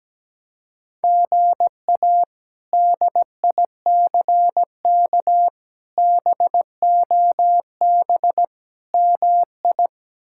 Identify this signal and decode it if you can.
Morse code, heard as a keyed tone, about 17 wpm, 700 Hz. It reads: GA DICK BOB MI
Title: K1USN SST QSO Practice - 17wpm